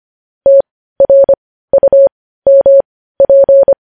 Morse code, heard as a keyed tone, about 25 wpm, spaced as for 15 wpm, 550 Hz. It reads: TRUMP
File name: podcast / News headlines-1x-25x15